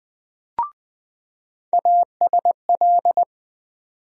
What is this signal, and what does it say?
Morse code, heard as a keyed tone, about 20 wpm, 700 Hz. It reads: ASL
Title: USA Call Sign Postfixes - 3 Letters 20wpm